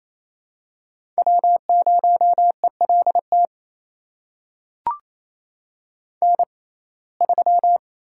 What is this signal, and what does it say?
Morse code, heard as a keyed tone, about 27 wpm, spaced as for 20 wpm, 700 Hz. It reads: W0ELT N3
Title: USA Call Signs - 5 Characters - 6x Character Spacing - 28wpm